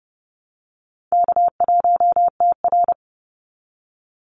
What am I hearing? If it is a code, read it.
Morse code, heard as a keyed tone, about 30 wpm, 700 Hz. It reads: K1TR
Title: Easy Call Signs Repeated 3x - 30wpm